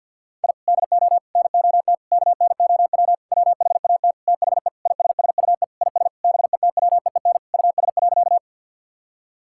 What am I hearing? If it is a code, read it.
Morse code, heard as a keyed tone, about 52 wpm, 700 Hz. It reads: I DO NOT KNOW WHAT THE ISSUE IS BETWEEN US.
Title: Sentences from Top 1000 Words - Review Part 6 - 50wpm